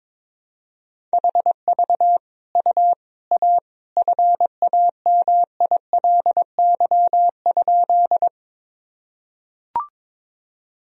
Morse code, heard as a keyed tone, about 22 wpm, 700 Hz. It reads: HV U A FAMILY?
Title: Sentences from Top 400 Words and Abbreviations 22wpm